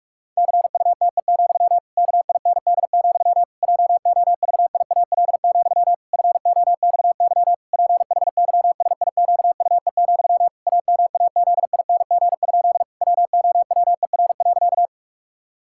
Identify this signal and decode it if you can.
Morse code, heard as a keyed tone, about 46 wpm, 700 Hz. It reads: CUTE, KIND, JOVIAL, FOXY PHYSIQUE, AMAZING? WOWER.
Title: Pangram Challenge v2 - 45wpm